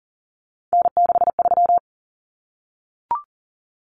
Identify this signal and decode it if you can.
Morse code, heard as a keyed tone, about 40 wpm, 700 Hz. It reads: N63
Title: Common Cut Numbers - 3-Digit 40wpm